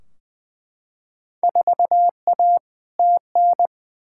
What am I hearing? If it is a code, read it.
Morse code, heard as a keyed tone, about 21 wpm, 700 Hz. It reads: 4A TN
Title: Field Day Exchanges - 20wpm